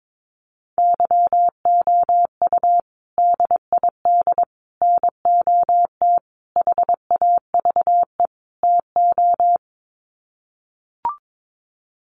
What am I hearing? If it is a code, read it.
Morse code, heard as a keyed tone, about 22 wpm, 700 Hz. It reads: YOU DID NOT HAVE TO
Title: Sets of 5 Words Encore 22wpm